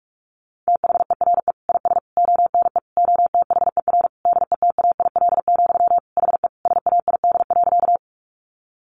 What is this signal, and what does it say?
Morse code, heard as a keyed tone, about 44 wpm, 700 Hz. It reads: THERE IS ONE OTHER DETAIL, HE SAID.
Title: Sentences from Top 1500 Words 45wpm